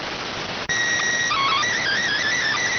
SSTV (slow-scan television), PD 290.